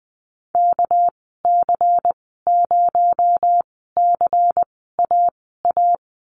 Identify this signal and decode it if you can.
Morse code, heard as a keyed tone, about 20 wpm, spaced as for 14 wpm, 700 Hz. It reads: KC0CAA
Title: 231 Lesson 37g - Letter x - Call Signs 20wpm